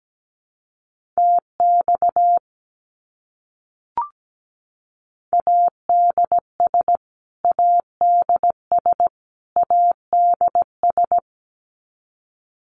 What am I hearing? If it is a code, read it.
Morse code, heard as a keyed tone, about 17 wpm, 700 Hz. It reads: TX ADS ADS ADS
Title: Abbreviations Repeated 3x - 17wpm